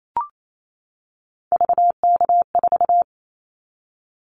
Morse code, heard as a keyed tone, about 28 wpm, 700 Hz. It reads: VK4
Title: Callsign Prefixes for Top 50 DXCC - 28wpm